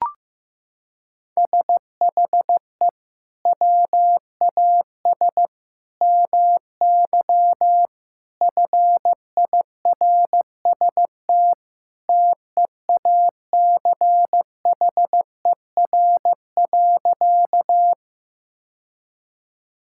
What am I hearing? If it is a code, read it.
Morse code, heard as a keyed tone, about 15 wpm, 700 Hz. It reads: SHE WAS MY FIRST TEACHER.
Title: Sentences from Top 1000 Words - Review Part 6 - 15wpm